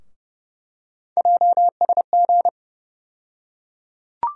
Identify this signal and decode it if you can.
Morse code, heard as a keyed tone, about 30 wpm, 700 Hz. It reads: JSG
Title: USA Call Sign Postfixes - 3 Letters 30wpm